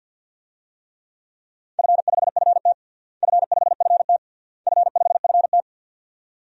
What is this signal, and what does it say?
Morse code, heard as a keyed tone, about 50 wpm, 700 Hz. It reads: F5FT F5FT F5FT
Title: DXCC-100 Call Signs - 4 Characters - Repeated 3x - 50wpm